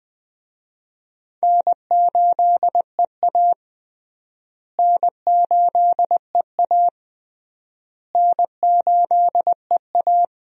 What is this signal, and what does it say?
Morse code, heard as a keyed tone, about 21 wpm, 700 Hz. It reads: N8EA N8EA N8EA
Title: USA Call Signs - 4 Characters - Repeated 3x - 20wpm